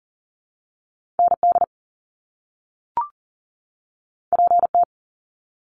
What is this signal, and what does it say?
Morse code, heard as a keyed tone, about 38 wpm, 700 Hz. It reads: ND PT
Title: Top Two Letter Combinations in Words 40wpm